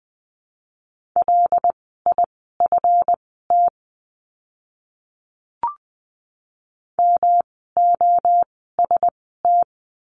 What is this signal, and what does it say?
Morse code, heard as a keyed tone, about 20 wpm, spaced as for 15 wpm, 700 Hz. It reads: LIFT MOST
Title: 139 Lesson 23f - Letter f - Words 20wpm